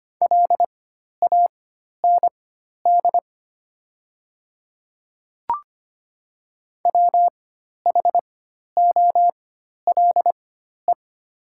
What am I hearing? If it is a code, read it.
Morse code, heard as a keyed tone, about 25 wpm, spaced as for 12 wpm, 700 Hz. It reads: LAND WHOLE
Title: 4x Character Spacing - Top 500 Words - 25wpm